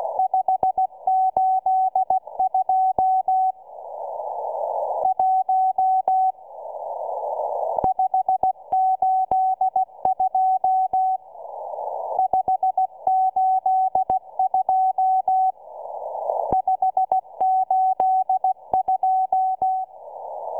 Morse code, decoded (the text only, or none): 582 1 582 582 582